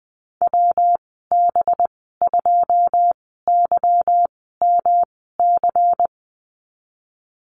Morse code, heard as a keyed tone, about 20 wpm, spaced as for 15 wpm, 700 Hz. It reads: WB2YMC
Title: 207 Lesson 33g - Letter j - Call Signs 20wpm